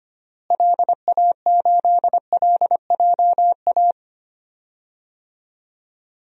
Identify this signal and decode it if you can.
Morse code, heard as a keyed tone, about 25 wpm, 700 Hz. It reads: LA8LJA
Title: DXCC-100 Call Signs - 6 Characters - Repeated 2x - 25wpm